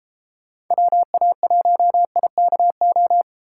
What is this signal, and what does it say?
Morse code, heard as a keyed tone, about 34 wpm, 700 Hz. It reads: WA1IKO
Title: Speed Racing - USA 2x3 Call Signs 28wpm